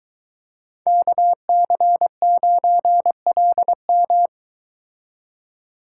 Morse code, heard as a keyed tone, about 22 wpm, 700 Hz. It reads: KC9LM